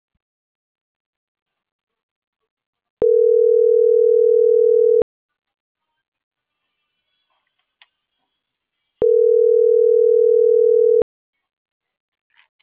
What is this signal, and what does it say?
Ringback tone (North America), 440 + 480 Hz.